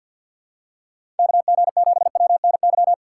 Morse code, heard as a keyed tone, about 52 wpm, 700 Hz. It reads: KG7WNY
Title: USA Call Signs - 6 Characters - 8x Character Spacing - 50wpm